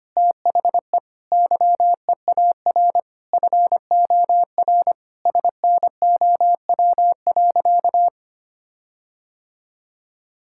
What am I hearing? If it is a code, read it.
Morse code, heard as a keyed tone, about 25 wpm, 700 Hz. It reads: THE YEAR FOR SNOW.
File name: practice / 25wpm / Sentences with 4-Word Phrases - 25wpm